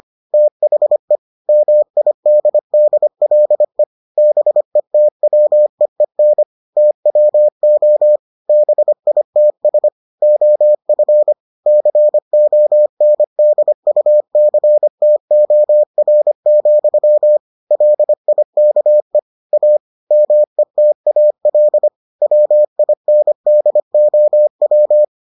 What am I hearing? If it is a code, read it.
Morse code, heard as a keyed tone, about 25 wpm, 600 Hz. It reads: THE MIDDLE BETWEEN TWO BITS OF CONDUCTOR, LIKE A METAL WINDOW